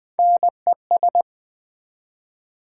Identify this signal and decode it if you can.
Morse code, heard as a keyed tone, about 20 wpm, 700 Hz. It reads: NES
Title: Build A Word - 1 Letter At A Time - 3 to 5-Letter Words - 20wpm